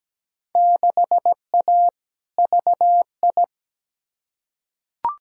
Morse code, heard as a keyed tone, about 17 wpm, 700 Hz. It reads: 6A VI
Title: Field Day Exchanges Encore - 17wpm